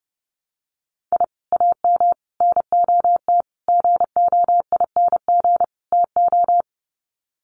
Morse code, heard as a keyed tone, about 30 wpm, 700 Hz. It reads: I AM NOT GOING TO
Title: Sets of 5 Words Encore 30wpm